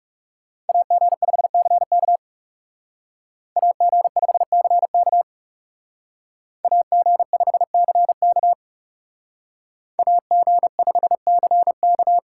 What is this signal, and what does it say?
Morse code, heard as a keyed tone, about 37 wpm, 700 Hz. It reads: AG5CK AG5CK AG5CK AG5CK
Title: Speed Racing - USA 2x2 Call Signs 30wpm